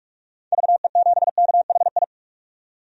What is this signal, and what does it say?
Morse code, heard as a keyed tone, about 41 wpm, 700 Hz. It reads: VE7KHI
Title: DXCC-100 Call Signs - 6 Characters - 45wpm